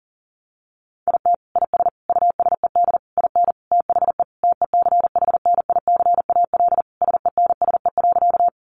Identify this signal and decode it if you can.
Morse code, heard as a keyed tone, about 40 wpm, 700 Hz. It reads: IT IS USED IN THE TECHNICAL SENSE.
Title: Sentences from Top 1100 Words 40wpm